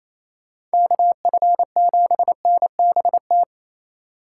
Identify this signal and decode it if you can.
Morse code, heard as a keyed tone, about 36 wpm, 700 Hz. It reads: KF7NBT